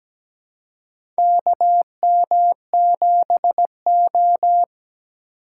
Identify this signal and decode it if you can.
Morse code, heard as a keyed tone, about 17 wpm, 700 Hz. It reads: KM7O
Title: USA Call Signs - 4 Characters - 2x Character Spacing - 17wpm